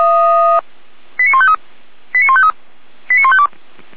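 Five-tone selective call (ZVEI), heard as quick runs of five tones, tones of about 70 ms. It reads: 89153 89153 89152
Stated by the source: ZVEI five-tone call received by SDR, logged by the Orlamünde fire brigade (feuerwehr-orlamuende).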